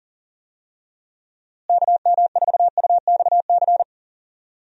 Morse code, heard as a keyed tone, about 40 wpm, 700 Hz. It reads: KM4UXC
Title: USA Call Signs - 6 Characters - Repeated 2x - 40wpm